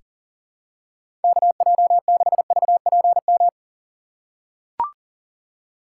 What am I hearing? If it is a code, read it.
Morse code, heard as a keyed tone, about 40 wpm, 700 Hz. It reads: KJ6VPM
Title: USA Call Signs - 2x Character Spacing - 40wpm